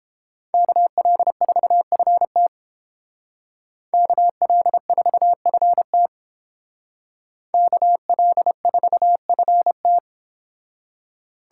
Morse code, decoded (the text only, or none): KL4FT KL4FT KL4FT